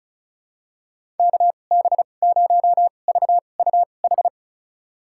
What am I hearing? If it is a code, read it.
Morse code, heard as a keyed tone, about 36 wpm, spaced as for 25 wpm, 700 Hz. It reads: KB0VUH
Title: USA Call Signs - 6 Characters - 2x Character Spacing - 35wpm